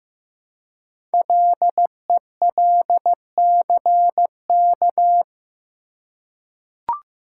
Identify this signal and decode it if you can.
Morse code, heard as a keyed tone, about 15 wpm, 700 Hz. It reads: LELCK
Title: Code Groups - 5 Letters - 15wpm